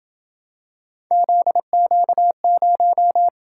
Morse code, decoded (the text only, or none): ZQ0